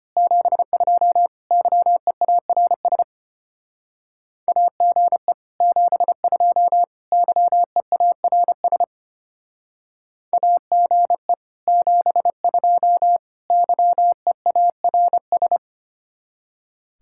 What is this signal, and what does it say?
Morse code, heard as a keyed tone, about 29 wpm, 700 Hz. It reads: 72 YEARS AGE 72 YEARS AGE 72 YEARS
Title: Speed-Racing Common QSO Phrases - 25wpm